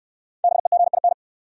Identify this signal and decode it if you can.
Morse code, heard as a keyed tone, about 63 wpm, 700 Hz. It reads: BEGIN